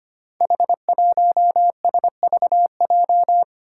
Morse code, heard as a keyed tone, about 25 wpm, 700 Hz. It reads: H1SVJ
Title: Five Challenging Letter-Number Combinations 25wpm